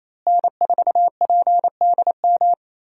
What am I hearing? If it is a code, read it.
Morse code, heard as a keyed tone, about 27 wpm, 700 Hz. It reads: N4PDM
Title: USA Call Signs - 5 Characters - Repeated 2x - 28wpm